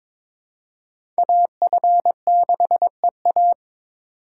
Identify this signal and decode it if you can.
Morse code, heard as a keyed tone, about 22 wpm, 700 Hz. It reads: AF6EA
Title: USA Call Signs - 5 Characters - 2x Character Spacing - 22wpm